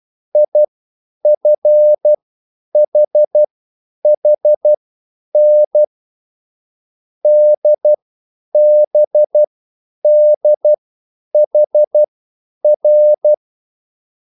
Morse code, heard as a keyed tone, about 12 wpm, spaced as for 9 wpm, 600 Hz. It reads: IFHHN DBDHR
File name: QSL / CW lesson